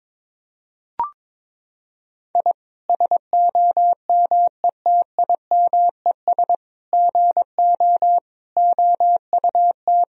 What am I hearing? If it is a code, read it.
Morse code, heard as a keyed tone, about 22 wpm, 700 Hz. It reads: I SOMETIMES GO OUT
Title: Sentences from Top 1000 Words - Review Part 3 - 22wpm